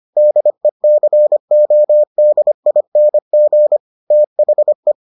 Morse code, heard as a keyed tone, about 25 wpm, 600 Hz. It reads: DECODING THE